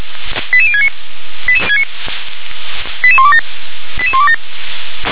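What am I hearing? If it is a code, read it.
Five-tone selective call (ZVEI), heard as quick runs of five tones, tones of about 70 ms. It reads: 88069 88127 88127
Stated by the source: Five-tone ZVEI selective call received on SDR.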